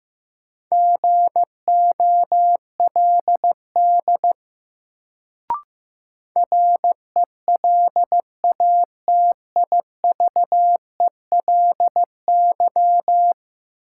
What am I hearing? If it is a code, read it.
Morse code, heard as a keyed tone, about 15 wpm, 700 Hz. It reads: GOLD RELATIVELY
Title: Sentences from Top 1100 Words - New Word Drill Alt - 15wpm